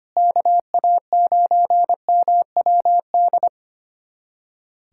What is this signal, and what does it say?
Morse code, heard as a keyed tone, about 25 wpm, 700 Hz. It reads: KA9MWD
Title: USA Callsign DE Callsign - 6 Characters - 25wpm